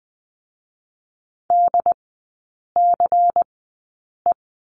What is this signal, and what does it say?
Morse code, heard as a keyed tone, about 20 wpm, 700 Hz. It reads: D C E
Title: 100 Lesson 17e - Letter u - Three Characters 20wpm